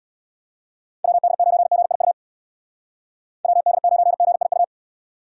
Morse code, heard as a keyed tone, about 63 wpm, 700 Hz. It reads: WD9GIU WD9GIU